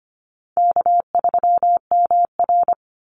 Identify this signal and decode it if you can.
Morse code, heard as a keyed tone, about 25 wpm, 700 Hz. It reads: K3MR
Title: Call Signs - 25wpm